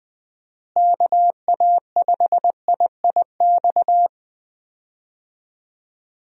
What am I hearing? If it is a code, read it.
Morse code, heard as a keyed tone, about 19 wpm, 700 Hz. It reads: KA5IIX
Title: USA Call Signs - 6 Characters - Repeated 3x - 20wpm